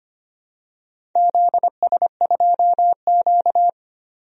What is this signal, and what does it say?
Morse code, heard as a keyed tone, about 25 wpm, 700 Hz. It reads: ZS2Q
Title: DXCC-100 Call Signs - 4 Characters - 25wpm